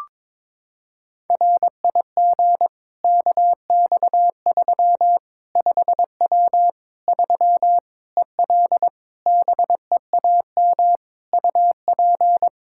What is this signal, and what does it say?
Morse code, heard as a keyed tone, about 22 wpm, 700 Hz. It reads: RIG KX3 5W 3 EL BEAM UP
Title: K1USN SST QSO Practice 22wpm